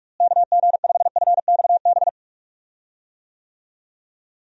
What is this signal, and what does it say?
Morse code, heard as a keyed tone, about 46 wpm, 700 Hz. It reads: KG5FXB
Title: USA Call Signs - 6 Characters - Repeated 3x - 45wpm